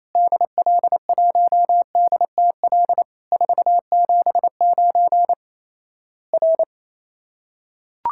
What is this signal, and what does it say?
Morse code, heard as a keyed tone, about 28 wpm, 700 Hz. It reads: DL1DTL 479 R
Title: Worked All Europe DX Contest - QTC Exchange - 28wpm